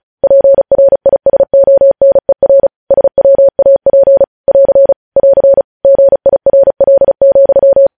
Morse code, heard as a keyed tone, about 36 wpm, 550 Hz. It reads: PRISONER SWAP <AR> <AR> GIRL,
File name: podcast / News headlines-1x-35x35